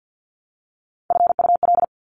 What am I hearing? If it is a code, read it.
Morse code, heard as a keyed tone, about 50 wpm, 700 Hz. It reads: FUL
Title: Top Three Letter Combinations in Words 50wpm